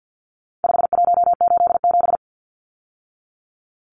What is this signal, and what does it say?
Morse code, heard as a keyed tone, about 48 wpm, 700 Hz. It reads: 5187
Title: Numbers - 4-Digit 50wpm